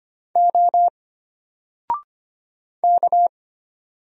Morse code, heard as a keyed tone, about 25 wpm, 700 Hz. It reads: O K
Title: Single Letter-Number 25wpm